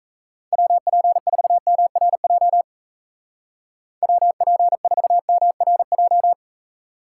Morse code, heard as a keyed tone, about 35 wpm, 700 Hz. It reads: WP4MRJ WP4MRJ
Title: Speed Racing - USA 2x3 Call Signs 28wpm